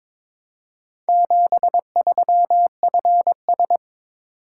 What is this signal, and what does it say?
Morse code, heard as a keyed tone, about 23 wpm, 700 Hz. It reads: 73FS
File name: practice / 22wpm / Four Letter-Number Combinations - 22wpm